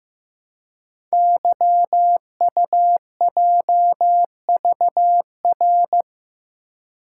Morse code, heard as a keyed tone, about 15 wpm, 700 Hz. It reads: YUJVR
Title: Code Groups - 5 Letters - 15wpm